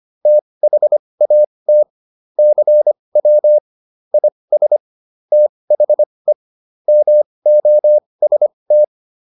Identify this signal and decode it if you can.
Morse code, heard as a keyed tone, about 25 wpm, spaced as for 20 wpm, 600 Hz. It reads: THAT CW IS THE MOST